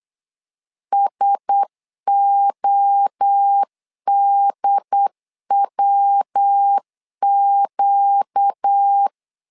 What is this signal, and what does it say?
Morse code, heard as a keyed tone, about 8 wpm, 800 Hz. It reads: SODWQ
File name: LEZIONI / 4 Q1S 8wpm 2710